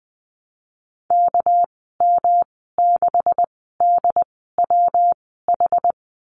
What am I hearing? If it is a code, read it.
Morse code, heard as a keyed tone, about 20 wpm, spaced as for 14 wpm, 700 Hz. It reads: KM6DWH
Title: 238 Lesson 38g - Letter q - Call Signs 20wpm